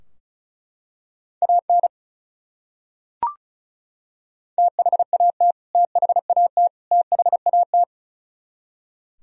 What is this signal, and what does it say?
Morse code, heard as a keyed tone, about 34 wpm, 700 Hz. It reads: AN THAT THAT THAT
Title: Top 100 Words Repeated 35wpm